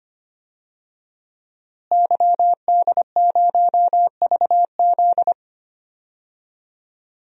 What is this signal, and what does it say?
Morse code, heard as a keyed tone, about 25 wpm, 700 Hz. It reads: YD0VZ
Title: DXCC-100 Call Signs - 5 Characters - Repeated 2x - 25wpm